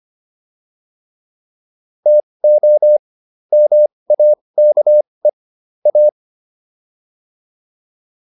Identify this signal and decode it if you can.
Morse code, heard as a keyed tone, about 25 wpm, spaced as for 20 wpm, 600 Hz. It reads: TO MAKE A
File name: WD202 20